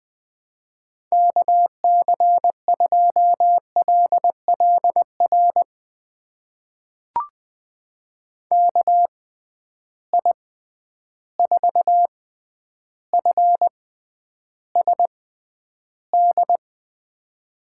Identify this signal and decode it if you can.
Morse code, heard as a keyed tone, about 20 wpm, spaced as for 10 wpm, 700 Hz. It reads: KC2LLR KI4FSD
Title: USA Call Signs - 6x Character Spacing - 20wpm